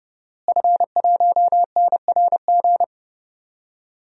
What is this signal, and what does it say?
Morse code, heard as a keyed tone, about 30 wpm, 700 Hz. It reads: F1NRG
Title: DXCC-100 Call Signs - 5 Characters - Repeated 3x - 30wpm